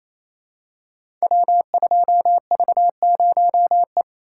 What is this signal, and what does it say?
Morse code, heard as a keyed tone, about 27 wpm, 700 Hz. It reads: W2V0E
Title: Code Groups - 5 Letter:Numbers - 28wpm